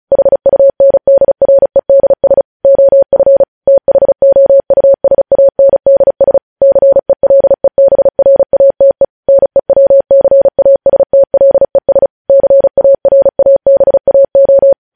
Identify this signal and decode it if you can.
Morse code, heard as a keyed tone, about 36 wpm, 550 Hz. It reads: HUNDREDS OF THOUSANDS CELEBRATE NEWCASTLES CARABAO
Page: News headlines in Morse code at 35 wpm